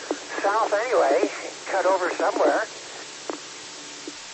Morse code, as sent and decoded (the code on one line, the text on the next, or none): none